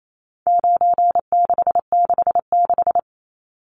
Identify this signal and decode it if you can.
Morse code, heard as a keyed tone, about 28 wpm, 700 Hz. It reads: 9666
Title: Numbers - 4-Digit 28wpm